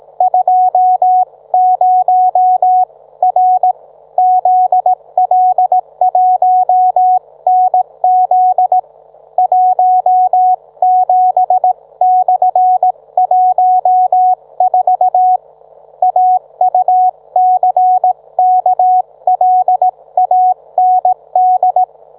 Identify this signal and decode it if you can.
Morse code, heard as a keyed tone, about 18 wpm, 740 Hz. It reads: 20R ZL1NZ 17/14 AUCKLAND